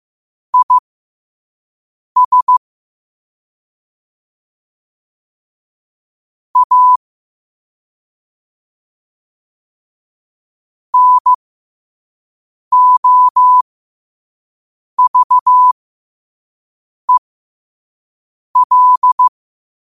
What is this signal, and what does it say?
Morse code, heard as a keyed tone, about 15 wpm, spaced as for 5 wpm, 1000 Hz. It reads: IS A NOVEL
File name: qotdfiles / 5wpm